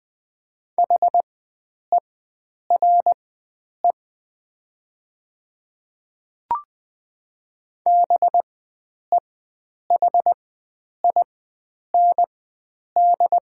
Morse code, heard as a keyed tone, about 20 wpm, spaced as for 9 wpm, 700 Hz. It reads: HERE BEHIND
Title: 4x Character Spacing - Top 500 Words - 20wpm